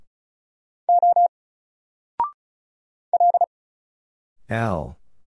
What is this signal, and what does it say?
Morse code, heard as a keyed tone, about 36 wpm, spaced as for 8 wpm, 700 Hz. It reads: O LT
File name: Single-Letter-Number-35wpm